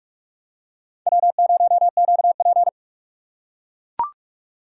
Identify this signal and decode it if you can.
Morse code, heard as a keyed tone, about 43 wpm, 700 Hz. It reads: W0QP
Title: USA Call Signs - 4 Characters - 8x Character Spacing - 45wpm